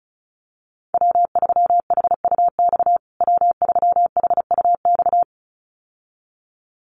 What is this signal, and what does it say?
Morse code, heard as a keyed tone, about 34 wpm, 700 Hz. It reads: W3HUX W3HUX
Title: USA Call Signs Repeated 2x 35wpm